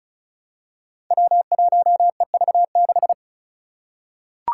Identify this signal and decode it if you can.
Morse code, heard as a keyed tone, about 36 wpm, 700 Hz. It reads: W1EV6
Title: Five Challenging Letter-Number Combinations 35wpm